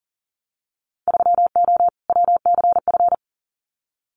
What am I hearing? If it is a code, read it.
Morse code, heard as a keyed tone, about 42 wpm, 700 Hz. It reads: 3O WCF
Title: Winter Field Day Exchanges 40wpm